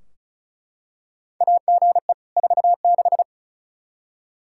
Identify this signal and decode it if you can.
Morse code, heard as a keyed tone, about 36 wpm, 700 Hz. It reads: AGE 46
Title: K1USN SST QSO Practice 35wpm